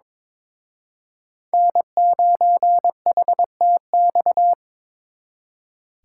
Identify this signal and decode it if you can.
Morse code, heard as a keyed tone, about 26 wpm, 700 Hz. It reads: N9HTX
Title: Speed Racing - USA 1x3 Call Signs 22wpm